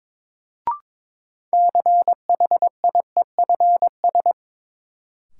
Morse code, heard as a keyed tone, about 21 wpm, 700 Hz. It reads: CHIEFS
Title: Sets of 5 Words Encore - New Word Drill - 22wpm